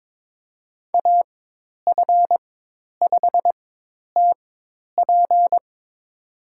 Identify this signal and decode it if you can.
Morse code, heard as a keyed tone, about 22 wpm, spaced as for 10 wpm, 700 Hz. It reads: AF5TP
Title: USA Call Signs - 5 Characters - 4x Character Spacing - 22wpm